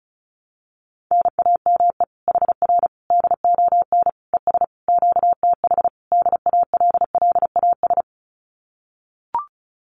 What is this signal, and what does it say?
Morse code, heard as a keyed tone, about 34 wpm, 700 Hz. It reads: NAME HR DON ES QTH DALLAS
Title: Common QSO Phrases Encore 35wpm